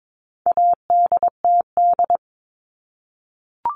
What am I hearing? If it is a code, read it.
Morse code, heard as a keyed tone, about 22 wpm, 700 Hz. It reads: ADTD
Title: All Cut Numbers - 4-Digit 22wpm